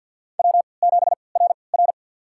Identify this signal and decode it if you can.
Morse code, heard as a keyed tone, about 53 wpm, spaced as for 29 wpm, 700 Hz. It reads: W7RR
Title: USA Call Signs - 4 Characters - 3x Character Spacing - 50wpm